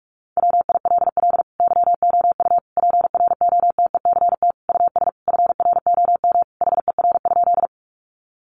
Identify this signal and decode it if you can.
Morse code, heard as a keyed tone, about 44 wpm, 700 Hz. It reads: WILL YOU PROTECT US FROM HER?
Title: Sentences from Top 900 Words 45wpm